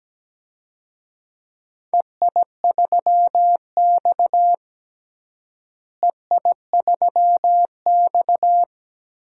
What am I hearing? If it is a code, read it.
Morse code, heard as a keyed tone, about 17 wpm, 700 Hz. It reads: EI3X EI3X
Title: DXCC-100 Call Signs - 4 Characters - Repeated 3x - 17wpm